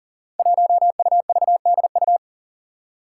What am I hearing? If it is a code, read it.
Morse code, heard as a keyed tone, about 37 wpm, 700 Hz. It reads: 1UVDU
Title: Five Challenging Letter-Number Combinations 40wpm